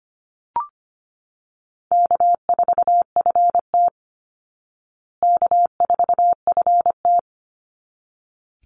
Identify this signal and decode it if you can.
Morse code, heard as a keyed tone, about 25 wpm, 700 Hz. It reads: K4FT K4FT
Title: Easy Call Signs Repeated 2x - 25wpm